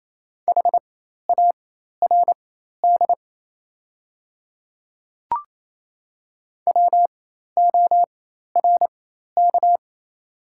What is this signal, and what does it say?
Morse code, heard as a keyed tone, about 28 wpm, spaced as for 13 wpm, 690 Hz. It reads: HARD WORK
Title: 4x Character Spacing - Top 300 Words - 28wpm